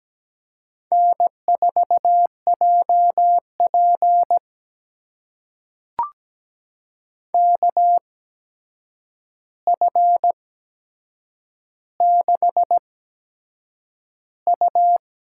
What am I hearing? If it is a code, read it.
Morse code, heard as a keyed tone, about 17 wpm, spaced as for 7 wpm, 700 Hz. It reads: N4JP KF6U
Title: USA Call Signs - 4 Characters - 8x Character Spacing - 17wpm